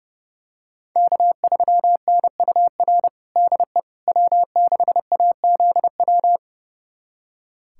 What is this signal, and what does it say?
Morse code, heard as a keyed tone, about 30 wpm, 700 Hz. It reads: K3NUR DE W6AZW
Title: USA Callsign DE Callsign - 5 Characters - 30wpm